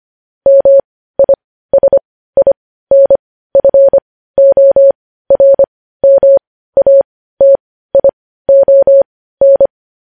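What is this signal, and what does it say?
Morse code, heard as a keyed tone, about 25 wpm, spaced as for 15 wpm, 550 Hz. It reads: MISINFORMATION
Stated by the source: Morse practice sending news headlines at 15 wpm.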